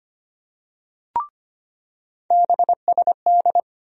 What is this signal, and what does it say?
Morse code, heard as a keyed tone, about 25 wpm, 700 Hz. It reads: BSD
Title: USA Call Sign Postfixes - 3 Letters 25wpm